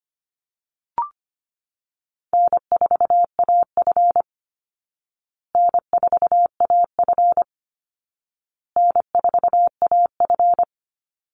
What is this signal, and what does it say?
Morse code, heard as a keyed tone, about 25 wpm, 700 Hz. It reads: N4AF N4AF N4AF
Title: Call Signs Repeated 2x - 25wpm